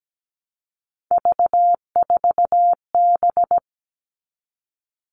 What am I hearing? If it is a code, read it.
Morse code, heard as a keyed tone, about 17 wpm, 700 Hz. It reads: V4B
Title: All Cut Numbers - 3-Digit 17wpm